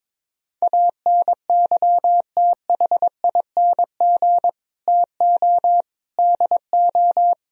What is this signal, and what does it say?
Morse code, heard as a keyed tone, about 22 wpm, 700 Hz. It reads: ANYTHING TO DO
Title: Speed-Racing - Sets of 4 Words 22wpm